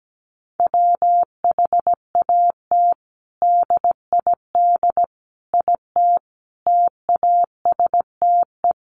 Morse code, heard as a keyed tone, about 17 wpm, 700 Hz. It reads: WHAT DID IT TASTE
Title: Sentences from Top 1100 Words 17wpm